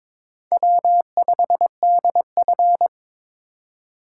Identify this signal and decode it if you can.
Morse code, heard as a keyed tone, about 22 wpm, 700 Hz. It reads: W5DF
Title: USA Call Signs - 4x Character Spacing - 22wpm